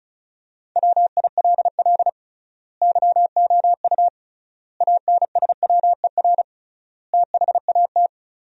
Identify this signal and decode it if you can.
Morse code, heard as a keyed tone, about 36 wpm, 700 Hz. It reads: WILL YOU ANSWER THAT
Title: 3x Word-Spacing - Sentences from Top 400 Words - 35wpm